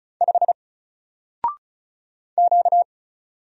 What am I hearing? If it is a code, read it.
Morse code, heard as a keyed tone, about 35 wpm, 700 Hz. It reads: 5 Q
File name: Single-Letter-Number-35wpm